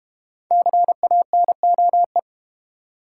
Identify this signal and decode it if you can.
Morse code, heard as a keyed tone, about 30 wpm, 700 Hz. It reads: CANOE
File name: Easy Words - Speed-Racing with Hints - 25wpm